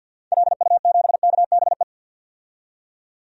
Morse code, heard as a keyed tone, about 47 wpm, 700 Hz. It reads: LU7KBE
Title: DXCC-100 Call Signs - 6 Characters - 50wpm